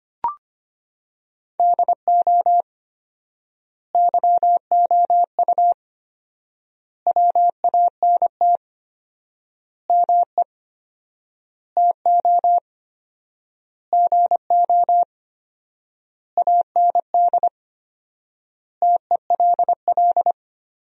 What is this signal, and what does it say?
Morse code, heard as a keyed tone, about 25 wpm, 700 Hz. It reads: DO YOU WANT ME TO GO AND TELL
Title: 4x Word-Spacing - Sentences from Top 200 Words - 25wpm